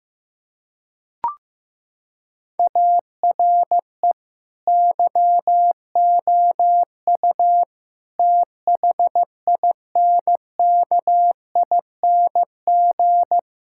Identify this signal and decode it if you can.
Morse code, heard as a keyed tone, about 15 wpm, 700 Hz. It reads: ARE YOU THINKING
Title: Sentences from Top 1000 Words - Review Part 6 - 15wpm